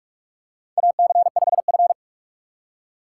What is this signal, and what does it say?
Morse code, heard as a keyed tone, about 42 wpm, 700 Hz. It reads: AC5F
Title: USA Call Signs - 4 Characters - 2x Character Spacing - 45wpm